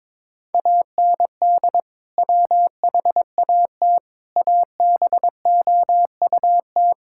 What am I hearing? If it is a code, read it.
Morse code, heard as a keyed tone, about 22 wpm, 700 Hz. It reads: AND WHAT ABOUT
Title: Sentences from Top 1000 Words - Review Part 6 - 22wpm